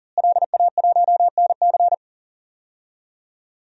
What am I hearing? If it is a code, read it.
Morse code, heard as a keyed tone, about 37 wpm, 700 Hz. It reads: LA1NC
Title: DXCC-100 Call Signs - 5 Characters - Repeated 2x - 40wpm